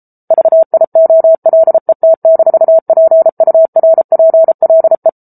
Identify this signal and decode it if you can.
Morse code, heard as a keyed tone, about 33 wpm, 650 Hz. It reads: VIOLET-PURPLE
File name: Book Two / 07-Chapter 7 - 33 WPM